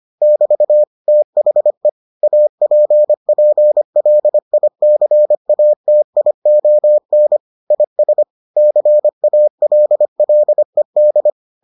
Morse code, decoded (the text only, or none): <BT> THE APPLICATION IS CALLED